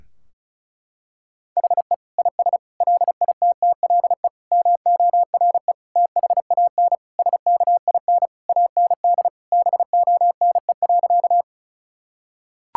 Morse code, heard as a keyed tone, about 36 wpm, 700 Hz. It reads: HE IS LITTLE MORE THAN SKIN AND BONE.
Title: Sentences from Top 1600 Words 35wpm